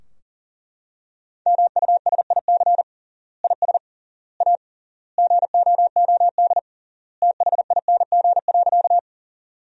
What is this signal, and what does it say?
Morse code, heard as a keyed tone, about 40 wpm, 700 Hz. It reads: MUSIC IS A GOOD THING.
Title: 3x Word-Spacing - Sentences from Top 300 Words - 40wpm